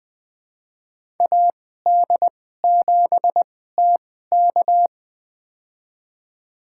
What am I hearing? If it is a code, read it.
Morse code, heard as a keyed tone, about 20 wpm, spaced as for 14 wpm, 700 Hz. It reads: AD7TK
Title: USA Call Signs - 2x Character Spacing - 20wpm